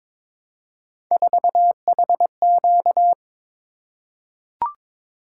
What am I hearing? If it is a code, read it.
Morse code, heard as a keyed tone, about 22 wpm, 700 Hz. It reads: 4HQ
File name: practice / 22wpm / Three Letter-Number Combinations - 22wpm